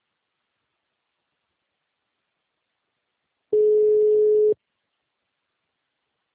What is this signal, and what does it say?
Ringback tone (Europe, 425 Hz).
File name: out- -2006- - - .84628